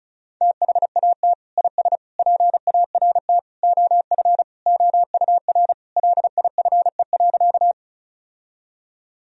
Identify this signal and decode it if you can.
Morse code, heard as a keyed tone, about 34 wpm, 700 Hz. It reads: THAT IS PART OF OUR LIFE.
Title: Sentences from Top 1000 Words - Review Part 1 - 35wpm